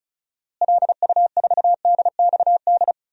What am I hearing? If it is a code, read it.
Morse code, heard as a keyed tone, about 36 wpm, 700 Hz. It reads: LU4DXD
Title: DXCC-100 Call Signs - 6 Characters - Repeated 3x - 35wpm